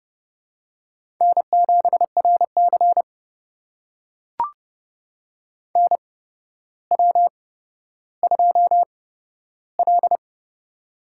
Morse code, heard as a keyed tone, about 30 wpm, spaced as for 13 wpm, 700 Hz. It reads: N7RC NW2L
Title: USA Call Signs - 4 Characters - 8x Character Spacing - 30wpm